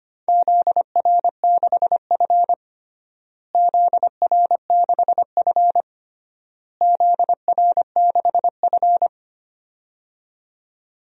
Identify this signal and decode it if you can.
Morse code, heard as a keyed tone, about 25 wpm, 700 Hz. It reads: ZR6F ZR6F ZR6F
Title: DXCC-100 Call Signs - 4 Characters - Repeated 3x - 25wpm